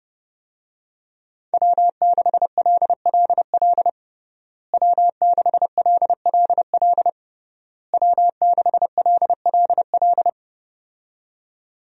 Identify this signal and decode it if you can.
Morse code, heard as a keyed tone, about 30 wpm, 700 Hz. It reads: W6LLL W6LLL W6LLL
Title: USA Call Signs - 5 Characters - Repeated 3x - 30wpm